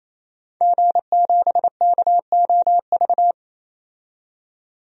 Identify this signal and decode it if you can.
Morse code, heard as a keyed tone, about 28 wpm, 700 Hz. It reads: G7KOV
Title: DXCC-100 Call Signs - 5 Characters - 28wpm